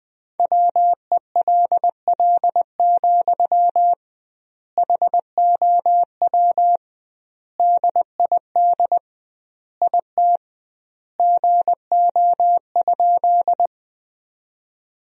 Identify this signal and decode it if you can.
Morse code, heard as a keyed tone, about 20 wpm, 700 Hz. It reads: WELL, HOW DID IT GO?
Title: 2x Word-Spacing - Sentences from Top 200 Words - 20wpm